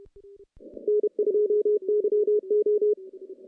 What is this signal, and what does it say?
Morse code, heard as a keyed tone, about 30 wpm, 410 Hz. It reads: N2YO
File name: contests / CW-SS-k5tr- -0000